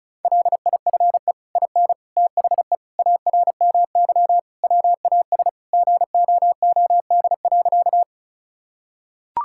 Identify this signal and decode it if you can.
Morse code, heard as a keyed tone, about 36 wpm, 700 Hz. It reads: LIFE IN THE ARMY WAS GOOD.
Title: Sentences from Top 1000 Words - Review Part 5 - 35wpm